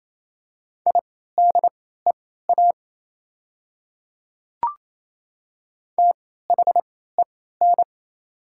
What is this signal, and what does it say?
Morse code, heard as a keyed tone, about 28 wpm, spaced as for 16 wpm, 700 Hz. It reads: IDEA THEN
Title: 3x Character Spacing - Top 300 Words - 28wpm